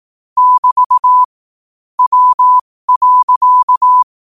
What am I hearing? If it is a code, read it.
Morse code, heard as a keyed tone, about 18 wpm, spaced as for 15 wpm, 1000 Hz. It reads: <BT> W.
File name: qotdfiles / 15wpm